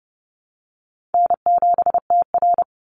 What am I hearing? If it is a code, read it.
Morse code, heard as a keyed tone, about 30 wpm, 700 Hz. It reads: N7TR
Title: Call Signs Repeated 2x - 30wpm